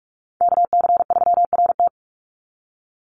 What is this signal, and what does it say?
Morse code, heard as a keyed tone, about 44 wpm, 700 Hz. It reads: KC3RT
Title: NA SOTA Chaser Callsigns 45wpm